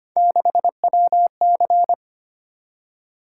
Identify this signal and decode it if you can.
Morse code, heard as a keyed tone, about 25 wpm, 700 Hz. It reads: 6WC
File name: Three Letter-Number Combinations - 25wpm